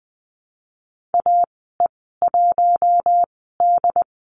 Morse code, heard as a keyed tone, about 20 wpm, spaced as for 14 wpm, 700 Hz. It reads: AE1D
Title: 082 Lesson 14g - Number 2 - Call Signs 20wpm